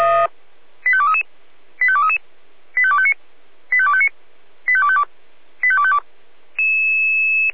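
Five-tone selective call (ZVEI), heard as quick runs of five tones, tones of about 80 ms. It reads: 85233 85233 85259 85259 85252 85252
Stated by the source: ZVEI five-tone selective call received on an SDR.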